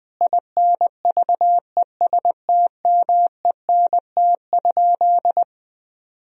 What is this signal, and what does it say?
Morse code, heard as a keyed tone, about 21 wpm, 700 Hz. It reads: INVESTMENT?
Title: Sentences from Top 1000 Words - Review Part 4 - 20wpm